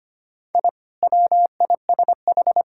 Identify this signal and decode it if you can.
Morse code, heard as a keyed tone, about 25 wpm, 700 Hz. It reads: I WISH
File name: practice / 25wpm / Sentences with 2-Word Phrases - 25wpm